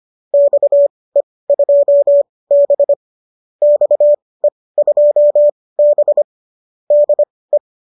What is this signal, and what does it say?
Morse code, heard as a keyed tone, about 25 wpm, spaced as for 18 wpm, 570 Hz. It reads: XE2B XE2B DE
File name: qso204 18